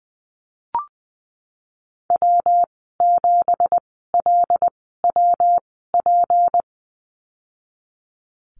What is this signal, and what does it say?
Morse code, heard as a keyed tone, about 20 wpm, spaced as for 15 wpm, 700 Hz. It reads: W7LWP
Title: 167 Lesson 27g - Number 7 - Call Signs 20wpm